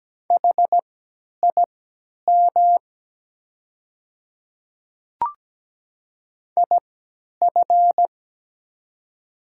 Morse code, heard as a keyed tone, about 17 wpm, spaced as for 10 wpm, 700 Hz. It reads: HIM IF
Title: 3x Character Spacing - Top 100 Words - 17wpm